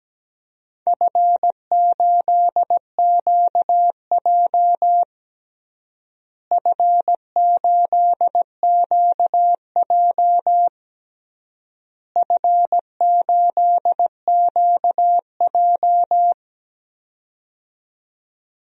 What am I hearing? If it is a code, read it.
Morse code, heard as a keyed tone, about 17 wpm, 700 Hz. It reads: F8QJ F8QJ F8QJ
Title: DXCC-100 Call Signs - 4 Characters - Repeated 3x - 17wpm